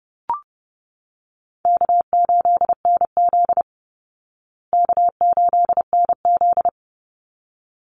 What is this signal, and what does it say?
Morse code, heard as a keyed tone, about 30 wpm, 700 Hz. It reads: K8NZ K8NZ
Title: Easy Call Signs Repeated 2x - 30wpm